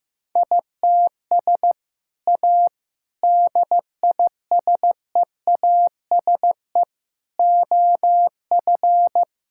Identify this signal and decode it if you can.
Morse code, heard as a keyed tone, about 15 wpm, 700 Hz. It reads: ITS A DISEASE OF
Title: Sentences from Top 1000 Words - Review Part 6 - 15wpm